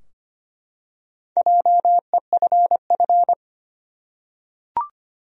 Morse code, heard as a keyed tone, about 25 wpm, 700 Hz. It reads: JEFF
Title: USA Common Names - Max 5 Letters - 25wpm